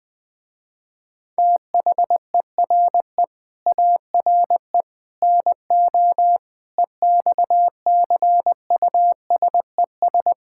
Morse code, heard as a keyed tone, about 20 wpm, 700 Hz. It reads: THERE ARE NO EXCUSES
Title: Sentences with 3-Word Phrases - Spoken Beginning - 20wpm